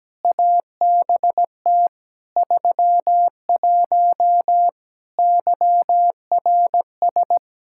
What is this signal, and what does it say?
Morse code, heard as a keyed tone, about 17 wpm, 700 Hz. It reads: ABT 31 YRS
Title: Practice QSO - Exchange 3 - More Info - 17wpm